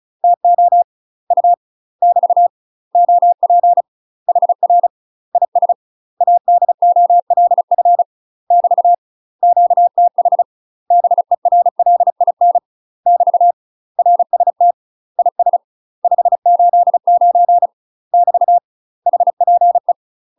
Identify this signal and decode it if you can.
Morse code, heard as a keyed tone, about 36 wpm, 700 Hz. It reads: TO U <BT> OP HR IS ADOLF <BT> QTH BERLIN <BT> RST IS 589 <BT> HPE
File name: UK MASTER QSOs TRAINING / 35WPM / UK MASTER QSOs 0011